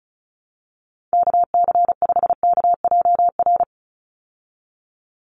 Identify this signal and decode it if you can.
Morse code, heard as a keyed tone, about 34 wpm, 700 Hz. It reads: KC5KJR
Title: USA Call Signs 35wpm